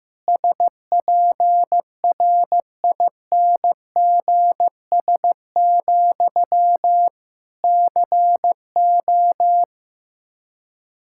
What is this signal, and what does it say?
Morse code, heard as a keyed tone, about 15 wpm, 700 Hz. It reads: SPRINGS, CO
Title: Practice QSO - Exchange 1 - Beginning - 15wpm